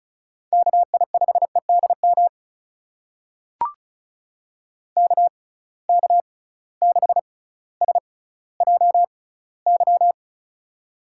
Morse code, heard as a keyed tone, about 36 wpm, spaced as for 18 wpm, 700 Hz. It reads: KI5EDM KK6SJY
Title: USA Call Signs - 6 Characters - 6x Character Spacing - 35wpm